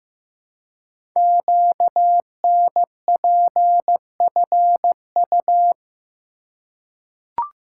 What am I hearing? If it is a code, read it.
Morse code, heard as a keyed tone, about 15 wpm, 700 Hz. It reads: QNPFU
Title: Code Groups - 5 Letters - 15wpm